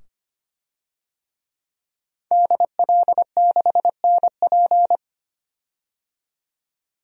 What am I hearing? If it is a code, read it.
Morse code, heard as a keyed tone, about 25 wpm, 700 Hz. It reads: DL6NP
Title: DXCC-100 Call Signs - 5 Characters - 25wpm